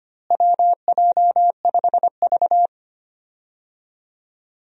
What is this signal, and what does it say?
Morse code, heard as a keyed tone, about 25 wpm, 700 Hz. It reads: WJ5V
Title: USA Call Signs - 4 Characters - Repeated 2x - 25wpm